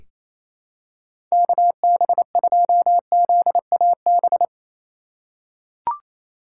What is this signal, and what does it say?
Morse code, heard as a keyed tone, about 28 wpm, 700 Hz. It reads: KB2ZAB
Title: USA Call Signs - 2x Character Spacing - 28wpm